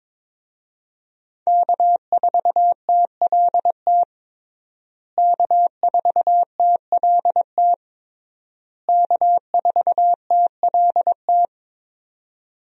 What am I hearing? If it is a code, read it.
Morse code, heard as a keyed tone, about 22 wpm, 700 Hz. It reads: K4TLT K4TLT K4TLT
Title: USA Call Signs - 5 Characters - Repeated 3x - 22wpm